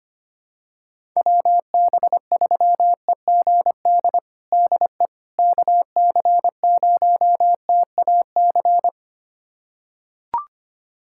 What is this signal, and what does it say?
Morse code, heard as a keyed tone, about 25 wpm, 700 Hz. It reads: WB3EGD DE KC0TAC
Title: USA Callsign DE Callsign - 6 Characters - 25wpm